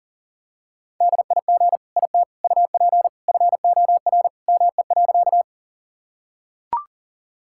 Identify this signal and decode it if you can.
Morse code, heard as a keyed tone, about 40 wpm, 700 Hz. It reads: DIG IT UP FOR ME.
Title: Sentences from Top 1600 Words 40wpm